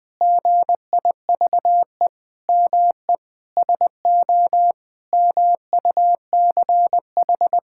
Morse code, heard as a keyed tone, about 20 wpm, 700 Hz. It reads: GIVE ME SO MUCH
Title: Sentences from Top 1600 Words 20wpm